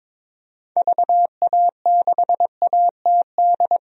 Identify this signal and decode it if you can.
Morse code, heard as a keyed tone, about 22 wpm, 700 Hz. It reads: VA6ATD
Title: DXCC-100 Call Signs - 6 Characters - 22wpm